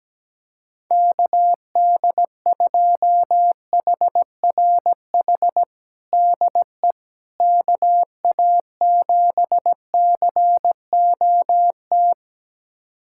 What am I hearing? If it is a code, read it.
Morse code, heard as a keyed tone, about 17 wpm, 700 Hz. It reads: KD2HRH DE KA7COT
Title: USA Callsign DE Callsign 17wpm